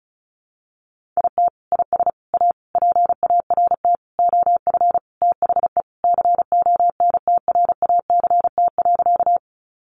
Morse code, heard as a keyed tone, about 34 wpm, 700 Hz. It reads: IT IS A PART OF THE CONTRACT.
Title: Sentences from Top 1400 Words 35wpm